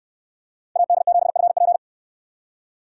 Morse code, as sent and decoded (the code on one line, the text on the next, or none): .- -.. --... ..-. .--.
AD7FP